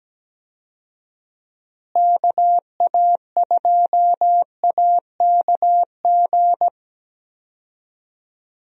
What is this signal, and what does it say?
Morse code, heard as a keyed tone, about 17 wpm, 700 Hz. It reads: KA2AKG
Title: USA Call Signs - 6 Characters - Repeated 3x - 17wpm